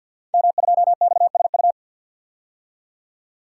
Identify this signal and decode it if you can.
Morse code, heard as a keyed tone, about 48 wpm, 700 Hz. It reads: M2XSU
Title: Code Groups - 5 Letter:Numbers - 50wpm